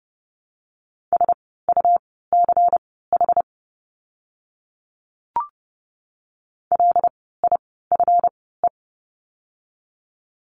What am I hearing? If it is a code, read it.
Morse code, heard as a keyed tone, about 30 wpm, spaced as for 17 wpm, 700 Hz. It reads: SUCH LIFE
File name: course / 30:10wpm / 230 Lesson 37f - Letter x - Words 30wpm